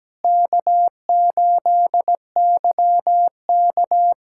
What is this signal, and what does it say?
Morse code, heard as a keyed tone, about 17 wpm, 700 Hz. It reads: K8YK
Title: USA Callsign DE Callsign - 4 Characters - 17wpm